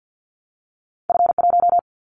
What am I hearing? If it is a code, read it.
Morse code, heard as a keyed tone, about 50 wpm, 700 Hz. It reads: F1
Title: Callsign Prefixes for Top 25 DXCC - 50wpm